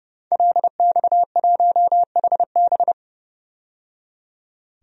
Morse code, heard as a keyed tone, about 31 wpm, 700 Hz. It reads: LX1HB
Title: DXCC-100 Call Signs - 5 Characters - 30wpm